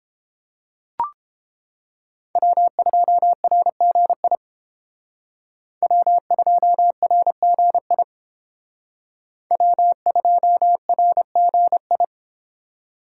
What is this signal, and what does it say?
Morse code, heard as a keyed tone, about 30 wpm, 700 Hz. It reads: W2RGI W2RGI W2RGI